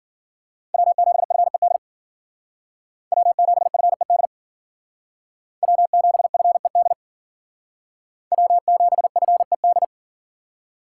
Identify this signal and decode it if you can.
Morse code, heard as a keyed tone, about 51 wpm, 700 Hz. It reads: W7FED W7FED W7FED W7FED